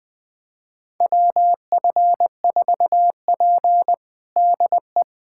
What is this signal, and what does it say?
Morse code, heard as a keyed tone, about 20 wpm, 700 Hz. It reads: WF4P DE
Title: USA Callsign DE Callsign - 4 Characters - 20wpm